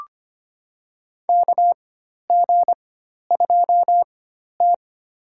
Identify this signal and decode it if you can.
Morse code, heard as a keyed tone, about 25 wpm, spaced as for 12 wpm, 690 Hz. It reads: KG2T
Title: USA Call Signs - 4 Characters - 4x Character Spacing - 25wpm